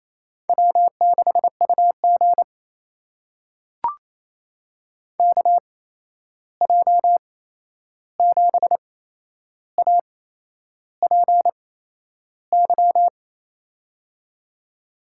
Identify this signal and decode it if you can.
Morse code, heard as a keyed tone, about 28 wpm, spaced as for 11 wpm, 700 Hz. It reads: W6UG KJ7APY
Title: USA Call Signs - 8x Character Spacing - 28wpm